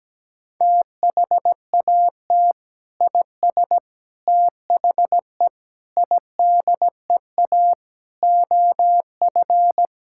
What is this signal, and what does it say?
Morse code, heard as a keyed tone, about 17 wpm, 700 Hz. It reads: THAT IS THE IDEA OF
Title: Sentences from Top 1000 Words - Review Part 5 - 17wpm